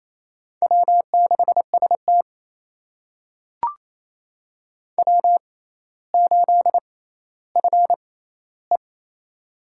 Morse code, heard as a keyed tone, about 28 wpm, spaced as for 14 wpm, 700 Hz. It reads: W6ST W8FE
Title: USA Call Signs - 4 Characters - 6x Character Spacing - 28wpm